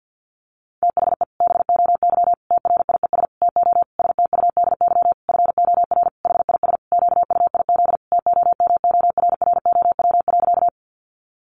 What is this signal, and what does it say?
Morse code, heard as a keyed tone, about 50 wpm, 700 Hz. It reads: THE BOY TRIES TO STUDY FOR HIS QUIZ TOMORROW.
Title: Additional Easy Phrases 50wpm